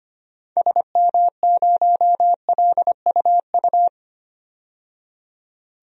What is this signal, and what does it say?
Morse code, heard as a keyed tone, about 25 wpm, 700 Hz. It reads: SM0LUU
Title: DXCC-100 Call Signs - 6 Characters - 25wpm